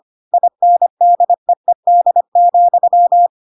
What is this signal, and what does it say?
Morse code, heard as a keyed tone, about 25 wpm, 700 Hz. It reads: INDEED,
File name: Pinocchio / Pinocchio ch5 25 25 ik1hge